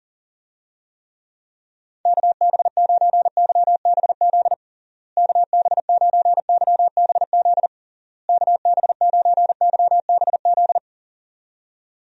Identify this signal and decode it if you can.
Morse code, heard as a keyed tone, about 40 wpm, 700 Hz. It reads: KB9YBZ KB9YBZ KB9YBZ
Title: USA Call Signs - 6 Characters - Repeated 3x - 40wpm